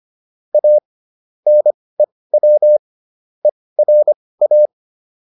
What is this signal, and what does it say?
Morse code, heard as a keyed tone, about 25 wpm, spaced as for 18 wpm, 600 Hz. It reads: A NEW ERA